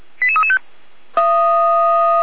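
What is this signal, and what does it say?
Five-tone selective call (ZVEI), heard as quick runs of five tones, tones of about 70 ms. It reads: 88335
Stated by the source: Five-tone ZVEI selective call received on SDR.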